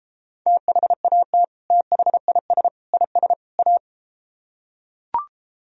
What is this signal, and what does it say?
Morse code, heard as a keyed tone, about 34 wpm, 700 Hz. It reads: THAT THIS IS A